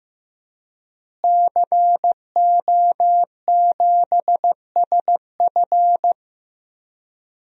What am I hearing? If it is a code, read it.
Morse code, heard as a keyed tone, about 15 wpm, 700 Hz. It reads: CO7SF
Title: DXCC-100 Call Signs - 5 Characters - Repeated 2x - 15wpm